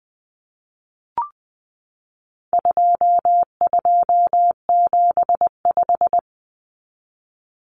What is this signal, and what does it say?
Morse code, heard as a keyed tone, about 20 wpm, 700 Hz. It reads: 2275
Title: Common Cut Numbers - 4-Digit 20wpm